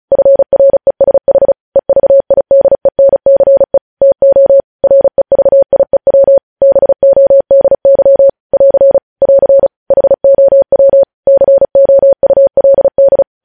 Morse code, heard as a keyed tone, about 36 wpm, 550 Hz. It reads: FRESH EVIDENCE TO REVIEW BODY <AR> <AR> HOW COULD
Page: news headlines in morse code at 35 wpm